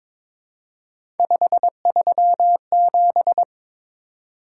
Morse code, heard as a keyed tone, about 23 wpm, 700 Hz. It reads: 537
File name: practice / 22wpm / Three Letter-Number Combinations - 22wpm